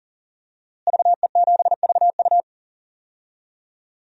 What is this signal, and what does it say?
Morse code, heard as a keyed tone, about 37 wpm, 700 Hz. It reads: VE7VU
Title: DXCC-100 Call Signs - 5 Characters - Repeated 3x - 40wpm